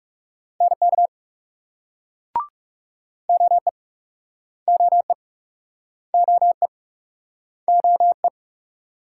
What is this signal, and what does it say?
Morse code, heard as a keyed tone, about 40 wpm, 700 Hz. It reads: NK OE OE OE OE